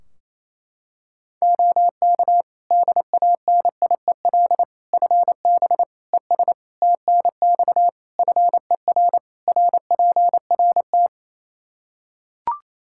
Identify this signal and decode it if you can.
Morse code, heard as a keyed tone, about 28 wpm, 700 Hz. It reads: OK DANIEL FB ES TNX FER RPRT
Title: K1USN SST QSO Practice 28wpm